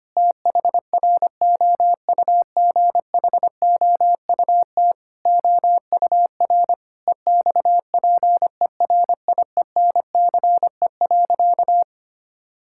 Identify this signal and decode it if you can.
Morse code, heard as a keyed tone, about 25 wpm, 700 Hz. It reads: THROUGHOUT OUR EXPERIENCE.
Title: Sentences from Top 1000 Words - Review Part 5 - 25wpm